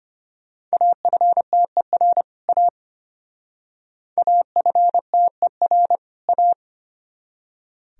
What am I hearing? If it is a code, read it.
Morse code, heard as a keyed tone, about 32 wpm, 700 Hz. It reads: AFTER A AFTER A